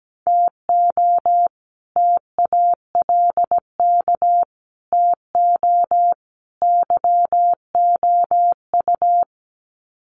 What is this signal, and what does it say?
Morse code, heard as a keyed tone, about 17 wpm, 700 Hz. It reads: TO TALK TO YOU
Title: Sets of 4 Words 17wpm